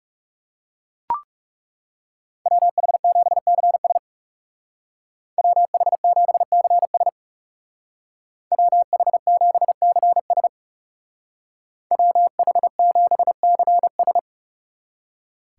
Morse code, heard as a keyed tone, about 39 wpm, 700 Hz. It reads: WH7CS WH7CS WH7CS WH7CS